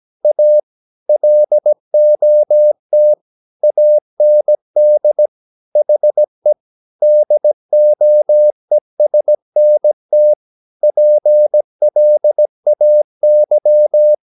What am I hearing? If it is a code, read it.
Morse code, heard as a keyed tone, about 17 wpm, 600 Hz. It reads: A LOT AND HE DOESNT PLAY
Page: Romanen Never af Ken Follet på engelsk.